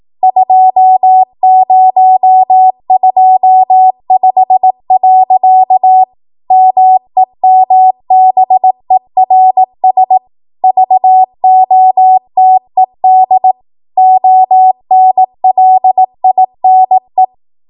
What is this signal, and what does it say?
Morse code, heard as a keyed tone, about 18 wpm, 750 Hz. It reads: 2025. MEMBERS VOTED ONLINE